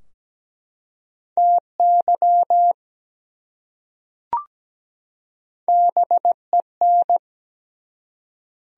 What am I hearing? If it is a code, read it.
Morse code, heard as a keyed tone, about 17 wpm, 700 Hz. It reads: TY BEN
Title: USA Common Names - Max 4 Letters - 17wpm